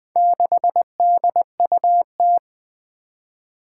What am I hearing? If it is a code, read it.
Morse code, heard as a keyed tone, about 20 wpm, 700 Hz. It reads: 6DUT
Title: Four Challenging Letter-Number Combinations 20wpm